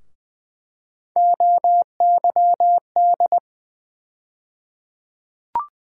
Morse code, heard as a keyed tone, about 21 wpm, 700 Hz. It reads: OYD
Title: USA Call Sign Postfixes - 3 Letters 20wpm